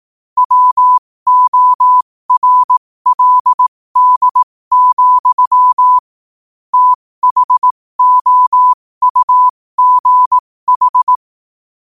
Morse code, heard as a keyed tone, about 18 wpm, spaced as for 15 wpm, 1000 Hz. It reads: WORLD, THOUGH
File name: qotdfiles / 15wpm